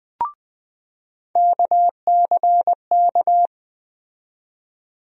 Morse code, heard as a keyed tone, about 20 wpm, 700 Hz. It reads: KCK
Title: USA Call Sign Postfixes - 3 Letters 20wpm